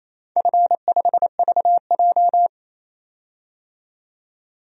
Morse code, heard as a keyed tone, about 28 wpm, 700 Hz. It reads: F5VJ
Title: DXCC-100 Call Signs - 4 Characters - 28wpm